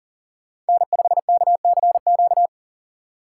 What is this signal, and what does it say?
Morse code, heard as a keyed tone, about 40 wpm, 700 Hz. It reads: N5KCQ
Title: USA Call Signs - 5 Characters - 2x Character Spacing - 40wpm